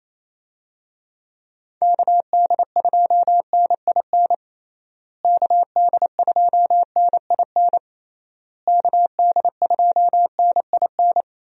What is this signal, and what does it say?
Morse code, heard as a keyed tone, about 28 wpm, 700 Hz. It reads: KD2NIN KD2NIN KD2NIN
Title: USA Call Signs - 6 Characters - Repeated 3x - 28wpm